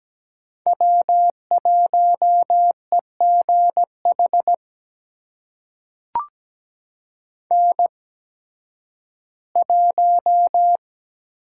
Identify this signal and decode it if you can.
Morse code, heard as a keyed tone, about 17 wpm, spaced as for 11 wpm, 700 Hz. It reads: W1EGH N1
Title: USA Call Signs - 5 Characters - 8x Character Spacing - 17wpm